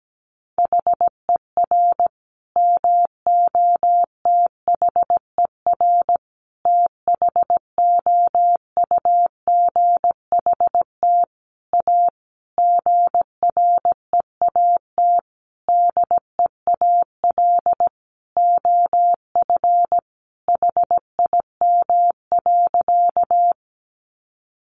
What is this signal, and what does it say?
Morse code, heard as a keyed tone, about 17 wpm, 700 Hz. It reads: HER MOTHER THOUGHT A GREAT DEAL OF HIM.
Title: Sentences from Top 800 Words 17wpm